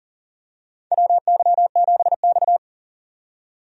Morse code, heard as a keyed tone, about 40 wpm, 700 Hz. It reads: WY7X
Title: USA Call Signs - 4 Characters - 40wpm